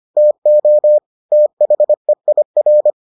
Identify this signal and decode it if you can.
Morse code, heard as a keyed tone, about 25 wpm, 600 Hz. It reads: TO THEIR